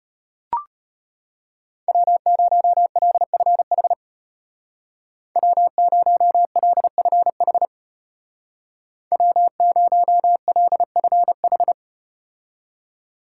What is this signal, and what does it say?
Morse code, heard as a keyed tone, about 34 wpm, 700 Hz. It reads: W0LFH W0LFH W0LFH